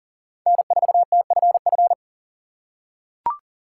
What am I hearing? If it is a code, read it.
Morse code, heard as a keyed tone, about 40 wpm, 700 Hz. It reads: N4TFF
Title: USA Call Signs - 5 Characters - 6x Character Spacing - 40wpm